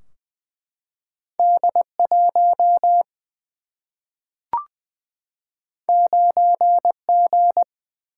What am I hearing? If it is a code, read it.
Morse code, heard as a keyed tone, about 20 wpm, 700 Hz. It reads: D 1 9 G
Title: Single-Letter Single-Number 20wpm